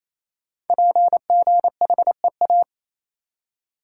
Morse code, heard as a keyed tone, about 29 wpm, 700 Hz. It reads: PGHEA
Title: Code Groups - 5 Letters - 28wpm